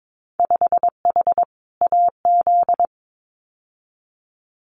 Morse code, heard as a keyed tone, about 22 wpm, 700 Hz. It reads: 5H AZ
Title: Winter Field Day Exchanges Encore 22wpm